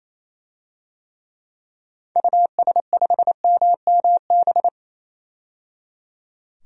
Morse code, heard as a keyed tone, about 28 wpm, 700 Hz. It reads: US5MMB
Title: DXCC-100 Call Signs - 6 Characters - 28wpm